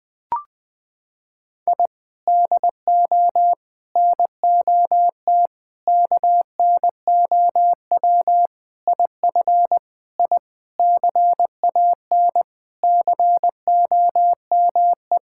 Morse code, heard as a keyed tone, about 20 wpm, 700 Hz. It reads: I DO NOT KNOW IF I CAN COME
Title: Sentences with 5-Word Phrases - 20wpm